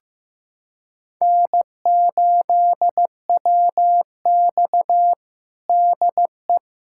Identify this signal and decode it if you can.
Morse code, heard as a keyed tone, about 15 wpm, 700 Hz. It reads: N8WX DE
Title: USA Callsign DE Callsign - 4 Characters - 15wpm